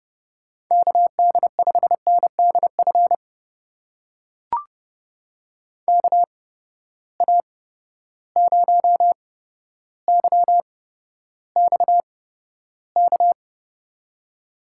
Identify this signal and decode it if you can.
Morse code, heard as a keyed tone, about 30 wpm, spaced as for 13 wpm, 700 Hz. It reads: KD5NDF KA0YXK
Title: USA Call Signs - 6 Characters - 8x Character Spacing - 30wpm